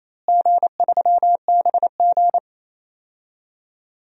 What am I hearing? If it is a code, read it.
Morse code, heard as a keyed tone, about 27 wpm, 700 Hz. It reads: G3BG
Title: DXCC-100 Call Signs - 4 Characters - Repeated 2x - 28wpm